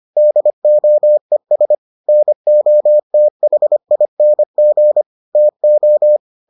Morse code, heard as a keyed tone, about 25 wpm, 600 Hz. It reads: DOES NOTHING TO